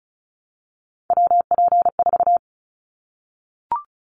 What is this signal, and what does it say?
Morse code, heard as a keyed tone, about 34 wpm, 700 Hz. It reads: WP4
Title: Callsign Prefixes for Top 25 DXCC - 35wpm